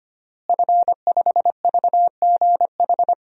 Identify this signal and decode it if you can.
Morse code, heard as a keyed tone, about 25 wpm, 700 Hz. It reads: F5VGH
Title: DXCC-100 Call Signs - 5 Characters - Repeated 3x - 25wpm